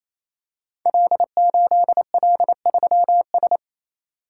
Morse code, heard as a keyed tone, about 27 wpm, 700 Hz. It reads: L8L3S
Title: Code Groups - 5 Letter:Numbers - 28wpm